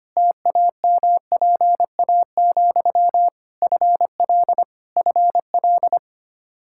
Morse code, heard as a keyed tone, about 25 wpm, 700 Hz. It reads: TAMPA, FL FL
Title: Practice QSO - Exchange 1 - Beginning - 25wpm